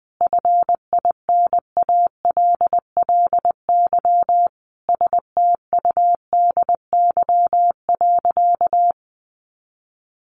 Morse code, heard as a keyed tone, about 20 wpm, 700 Hz. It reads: FINALLY STUDY.
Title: Additional Easy Phrases 20wpm